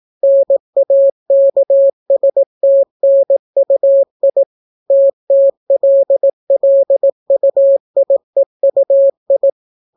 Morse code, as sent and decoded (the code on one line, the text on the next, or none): -. .- -.- ... - -. ..- .. / - - .-.. .-.. ..- .. ..- ..
NAKSTNUI TTLLUIEUI